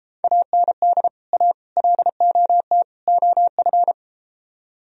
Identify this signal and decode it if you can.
Morse code, heard as a keyed tone, about 32 wpm, 700 Hz. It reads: AND A LOT OF